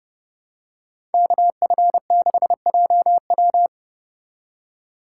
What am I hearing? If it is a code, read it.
Morse code, heard as a keyed tone, about 30 wpm, 700 Hz. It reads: KF6JW